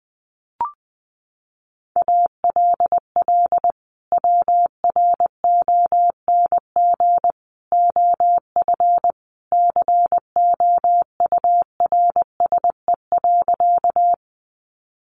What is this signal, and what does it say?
Morse code, heard as a keyed tone, about 20 wpm, 700 Hz. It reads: ALL WRONG OF COURSE.
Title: Sentences from Top 900 Words 20wpm